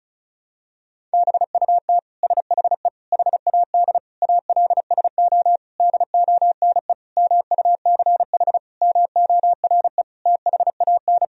Morse code, heard as a keyed tone, about 34 wpm, 700 Hz. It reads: BUT SHE HAD ALSO DONE MUCH MORE THAN
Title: Sentences from Top 1000 Words - Review Part 1 - 35wpm